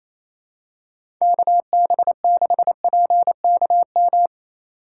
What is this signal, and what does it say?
Morse code, heard as a keyed tone, about 27 wpm, 700 Hz. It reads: KB6PKM
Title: USA Call Signs - 6 Characters - 6x Character Spacing - 28wpm